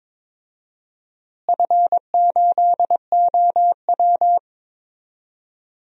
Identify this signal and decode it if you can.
Morse code, heard as a keyed tone, about 22 wpm, 700 Hz. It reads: F8OW
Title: DXCC-100 Call Signs - 4 Characters - Repeated 3x - 22wpm